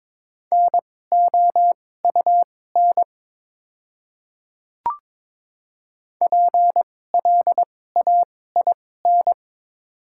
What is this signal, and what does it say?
Morse code, heard as a keyed tone, about 22 wpm, spaced as for 16 wpm, 700 Hz. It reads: NOUN PLAIN
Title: 2x Character Spacing - Top 500 Words - 22wpm